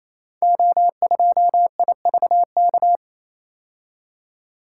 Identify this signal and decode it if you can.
Morse code, heard as a keyed tone, about 29 wpm, 700 Hz. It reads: O2IVK
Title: Code Groups - 5 Letter:Numbers - 28wpm